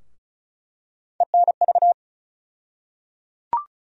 Morse code, heard as a keyed tone, about 36 wpm, 700 Hz. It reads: ENV
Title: USA Call Sign Postfixes - 3 Letters 35wpm